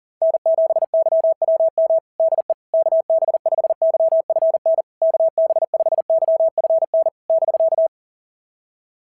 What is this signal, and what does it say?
Morse code, heard as a keyed tone, about 38 wpm, 660 Hz. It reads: N7YWM DE KB5YFN KB5YFN <BK>
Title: SOTA QSO Practice 40wpm